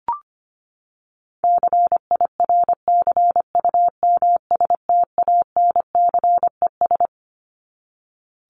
Morse code, heard as a keyed tone, about 25 wpm, 700 Hz. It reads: CIRCUMSTANCES
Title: ES:S Suffix - Top 100 Words with Base Words 25wpm